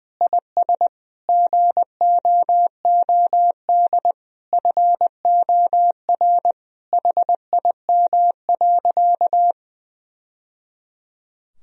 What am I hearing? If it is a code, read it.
Morse code, heard as a keyed tone, about 19 wpm, 700 Hz. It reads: IS GOOD FOR HIM.
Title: Sentences from Top 1000 Words - Review Part 1 - 20wpm